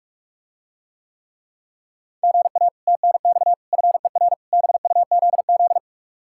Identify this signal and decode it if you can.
Morse code, heard as a keyed tone, about 44 wpm, 700 Hz. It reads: GA TNX FER BUZZ